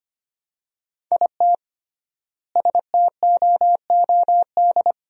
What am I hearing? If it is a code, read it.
Morse code, heard as a keyed tone, about 25 wpm, 700 Hz. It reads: IT STOOD
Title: 3x Word-Spacing - Sentences from Top 500 Words - 25wpm